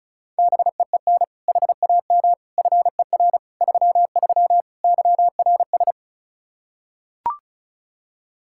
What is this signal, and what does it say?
Morse code, heard as a keyed tone, about 36 wpm, 700 Hz. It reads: BEEN HAM FER 33 YRS
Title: Practice QSO - Exchange 3 - More Info - 35wpm